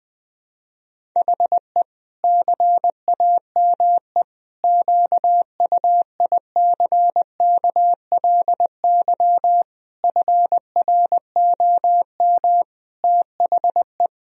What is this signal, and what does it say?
Morse code, heard as a keyed tone, about 21 wpm, 700 Hz. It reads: HE CAME QUICKLY FROM THE
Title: Sentences from Top 1000 Words - Review Part 1 - 20wpm